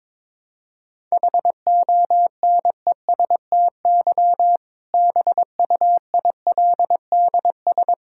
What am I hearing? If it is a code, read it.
Morse code, heard as a keyed tone, about 22 wpm, 700 Hz. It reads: HONESTY BUILDS
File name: Difficult Sentences with Hints - 22wpm